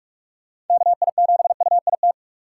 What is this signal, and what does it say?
Morse code, heard as a keyed tone, about 46 wpm, 700 Hz. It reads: KI7UIT
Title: USA Call Signs - 6 Characters - 2x Character Spacing - 45wpm